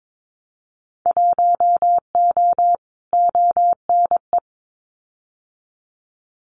Morse code, heard as a keyed tone, about 22 wpm, 700 Hz. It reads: 1O ONE
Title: Winter Field Day Exchanges 22wpm